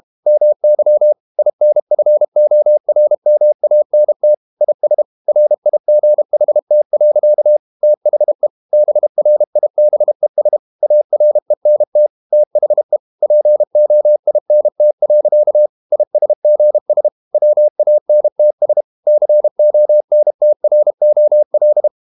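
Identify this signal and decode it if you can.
Morse code, heard as a keyed tone, about 32 wpm, 600 Hz. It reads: MY INFORMANT IS RIGHT. THE BRIBES ARENT THE POINT. ISGS WANTS CONTROL
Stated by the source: Morse code practice at 32 wpm from CW med Gnister, reading Ken Follett's novel Never in English.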